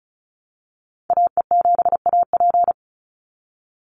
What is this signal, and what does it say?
Morse code, heard as a keyed tone, about 34 wpm, 700 Hz. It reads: AE7AP
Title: NA SOTA Activator Callsigns 35wpm